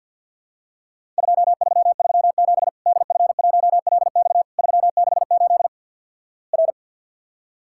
Morse code, heard as a keyed tone, about 50 wpm, 700 Hz. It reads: 2337 DF1LX 368 R
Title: Worked All Europe DX Contest - QTC Exchange with Repeats - 50wpm